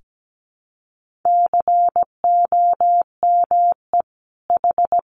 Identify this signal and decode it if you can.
Morse code, heard as a keyed tone, about 17 wpm, 700 Hz. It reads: COME H
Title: Sentences from Top 1500 Words 17wpm